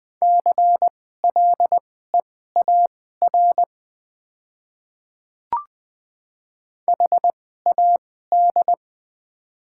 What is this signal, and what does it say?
Morse code, heard as a keyed tone, about 20 wpm, spaced as for 15 wpm, 700 Hz. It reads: CLEAR HAD
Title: 2x Character Spacing - Top 500 Words - 20wpm